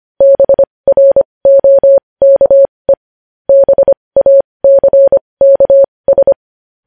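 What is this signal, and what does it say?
Morse code, heard as a keyed tone, about 25 wpm, spaced as for 20 wpm, 550 Hz. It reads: BROKE BACKS
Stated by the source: Morse code news headlines practice at 20 wpm.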